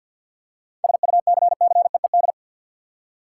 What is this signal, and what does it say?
Morse code, heard as a keyed tone, about 52 wpm, 700 Hz. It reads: SUCCEED
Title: Seven Letter Words - 50wpm